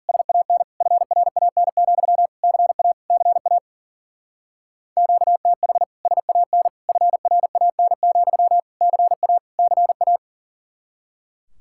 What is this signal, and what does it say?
Morse code, heard as a keyed tone, about 43 wpm, 700 Hz. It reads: SAN FRAN, CA CA QTH SAN FRAN, CA CA
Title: Speed-Racing Common QSO Phrases - 40wpm